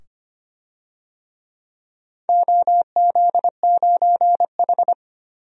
Morse code, heard as a keyed tone, about 25 wpm, 700 Hz. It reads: OZ9H
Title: DXCC-100 Call Signs - 4 Characters - 25wpm